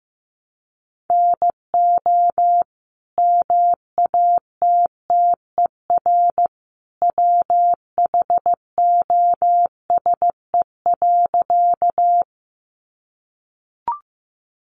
Morse code, heard as a keyed tone, about 15 wpm, 700 Hz. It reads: NO MATTER WHOSE.
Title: Sentences from Top 1200 Words 15wpm